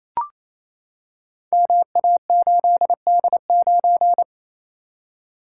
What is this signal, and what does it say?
Morse code, heard as a keyed tone, about 28 wpm, 700 Hz. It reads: MA8D9
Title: Code Groups - 5 Letter:Numbers - 28wpm